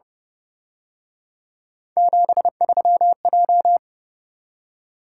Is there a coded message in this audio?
Yes: Morse code from 2.0 to 3.8 s, heard as a keyed tone, about 29 wpm, 700 Hz. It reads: 73J